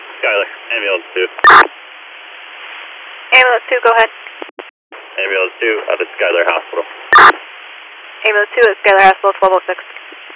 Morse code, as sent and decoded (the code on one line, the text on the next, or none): none